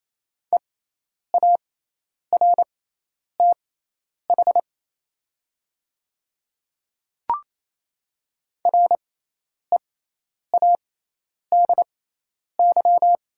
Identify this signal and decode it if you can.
Morse code, heard as a keyed tone, about 28 wpm, spaced as for 10 wpm, 700 Hz. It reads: EARTH READY
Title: 6x Character Spacing - Top 500 Words - 28wpm